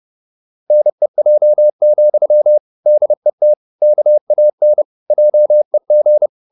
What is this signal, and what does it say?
Morse code, heard as a keyed tone, about 30 wpm, 600 Hz. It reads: NEJ, DET KAN JEG
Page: Ferie 30 wpm